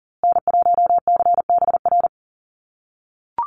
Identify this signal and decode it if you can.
Morse code, heard as a keyed tone, about 40 wpm, 700 Hz. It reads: N1CBR
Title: USA Call Signs Repeated 3x 40wpm